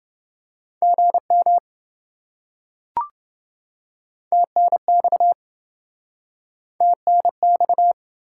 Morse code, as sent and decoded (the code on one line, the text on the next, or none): --. -- / - -. -..- / - -. -..-
GM TNX TNX